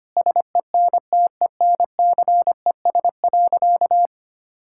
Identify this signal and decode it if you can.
Morse code, heard as a keyed tone, about 25 wpm, 700 Hz. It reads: SENTENCES.
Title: Pangram Challenge v2 - 25wpm